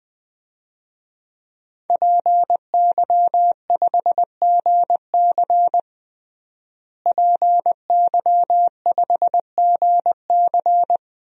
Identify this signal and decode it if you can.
Morse code, heard as a keyed tone, about 20 wpm, 700 Hz. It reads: PY5GC PY5GC
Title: DXCC-100 Call Signs - 5 Characters - Repeated 2x - 20wpm